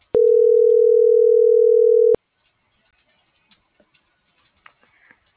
Ringback tone (North America), 440 + 480 Hz.